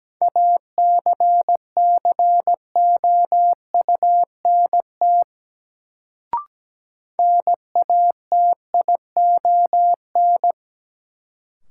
Morse code, heard as a keyed tone, about 17 wpm, 700 Hz. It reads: ACCOUNT NATION
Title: Sentences from Top 900 Words - New Word Drill - 17wpm